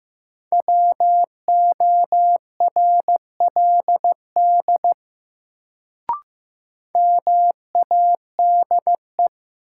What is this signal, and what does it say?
Morse code, heard as a keyed tone, about 15 wpm, 700 Hz. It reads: WORLD MADE
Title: Sentences from Top 200 Words - New Word Drill - 15wpm